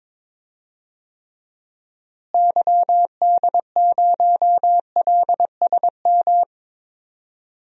Morse code, heard as a keyed tone, about 21 wpm, 700 Hz. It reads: YD0LSM
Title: DXCC-100 Call Signs - 6 Characters - Repeated 3x - 22wpm